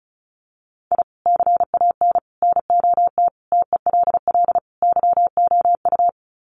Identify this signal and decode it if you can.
Morse code, heard as a keyed tone, about 34 wpm, 700 Hz. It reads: I CAN NOT TELL YOU
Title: Sets of 5 Words 35wpm